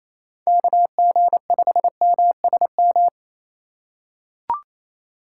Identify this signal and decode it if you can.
Morse code, heard as a keyed tone, about 28 wpm, 700 Hz. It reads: KG5MSM
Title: USA Call Signs - 6 Characters - 3x Character Spacing - 28wpm